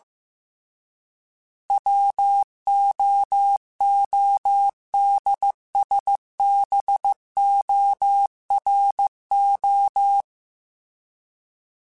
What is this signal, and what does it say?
Morse code, heard as a keyed tone, about 15 wpm, 800 Hz. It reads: WOODSBORO